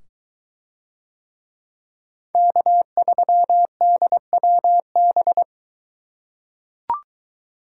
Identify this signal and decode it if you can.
Morse code, heard as a keyed tone, about 21 wpm, 700 Hz. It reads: K3DWB